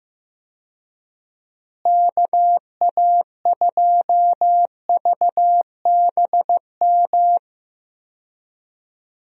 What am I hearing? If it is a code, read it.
Morse code, heard as a keyed tone, about 15 wpm, 700 Hz. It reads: KA2VBM
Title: USA Call Signs - 6 Characters - Repeated 2x - 15wpm